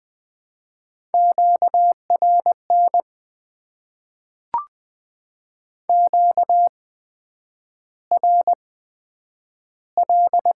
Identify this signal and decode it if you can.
Morse code, heard as a keyed tone, about 20 wpm, spaced as for 9 wpm, 700 Hz. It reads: QRN QRL
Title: Top 90 QSO Elements - 8x Character Spacing - 20wpm